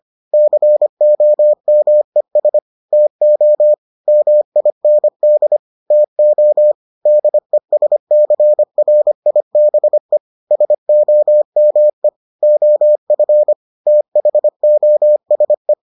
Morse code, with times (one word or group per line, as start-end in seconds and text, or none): 0.3-2.6 COMES
2.9-3.7 TO
4.1-5.6 MIND
5.9-6.7 TO
7.1-10.2 DESCRIBE
10.5-12.1 SOME
12.4-13.5 OF
13.9-15.7 THOSE